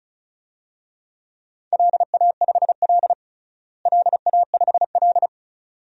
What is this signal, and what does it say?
Morse code, heard as a keyed tone, about 36 wpm, 700 Hz. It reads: LA5L LA5L
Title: DXCC-100 Call Signs - 4 Characters - Repeated 2x - 35wpm